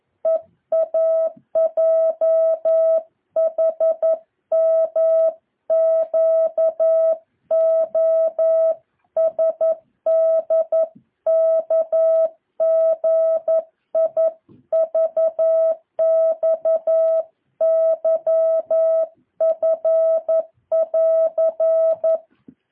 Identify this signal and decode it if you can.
Morse code, heard as a keyed tone, about 11 wpm, 650 Hz. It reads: EAJHMQOSDKGIVXYF<AR>
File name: jh8cbh / 2023cw